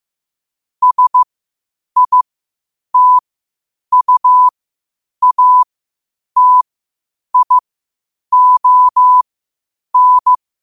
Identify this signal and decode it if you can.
Morse code, heard as a keyed tone, about 15 wpm, spaced as for 8 wpm, 1000 Hz. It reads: SITUATION
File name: qotdfiles / 8wpm